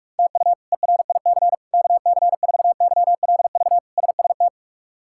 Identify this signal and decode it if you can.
Morse code, heard as a keyed tone, about 43 wpm, 700 Hz. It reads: TU ERIC KC4YLV SST
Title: K1USN SST QSO Practice - 45wpm